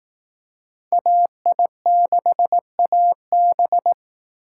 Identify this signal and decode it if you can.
Morse code, heard as a keyed tone, about 18 wpm, 700 Hz. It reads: AI6AB
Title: Speed Racing - USA 2x2 Call Signs 15wpm